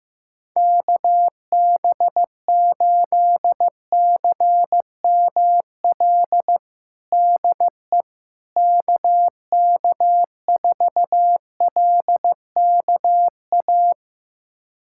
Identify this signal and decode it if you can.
Morse code, heard as a keyed tone, about 15 wpm, 700 Hz. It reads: KB8CML DE KK4LKA
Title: USA Callsign DE Callsign 15wpm